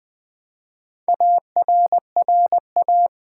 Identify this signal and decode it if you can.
Morse code, heard as a keyed tone, about 20 wpm, 700 Hz. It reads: ARRA
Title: Build A Word - 1 Letter At A Time - 3 to 5-Letter Words - 20wpm